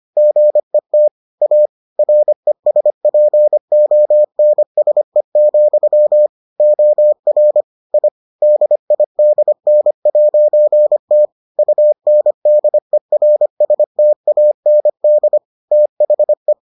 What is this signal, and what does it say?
Morse code, heard as a keyed tone, about 25 wpm, 600 Hz. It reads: GET A RESPONSE, OR I DIDN'T UNDERSTAND THE